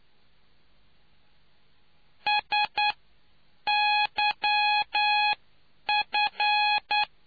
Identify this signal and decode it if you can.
Morse code, heard as a keyed tone, about 9 wpm, spaced as for 7 wpm, 840 Hz. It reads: SYF